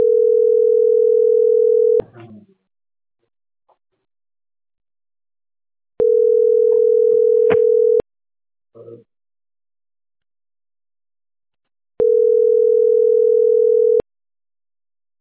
Ringback tone (North America), 440 + 480 Hz.